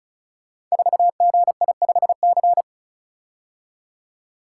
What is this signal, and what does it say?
Morse code, heard as a keyed tone, about 36 wpm, 690 Hz. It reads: 4GI5C
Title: Code Groups - 5 Letter:Numbers - 35wpm